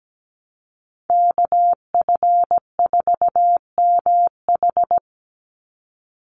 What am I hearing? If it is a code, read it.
Morse code, heard as a keyed tone, about 17 wpm, 700 Hz. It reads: KF4MH
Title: NA SOTA Chaser Callsigns 17wpm